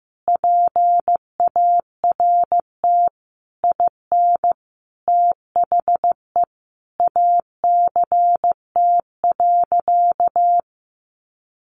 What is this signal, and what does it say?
Morse code, heard as a keyed tone, about 15 wpm, 700 Hz. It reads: PART IN THE ACT.
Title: Sentences from Top 1100 Words 15wpm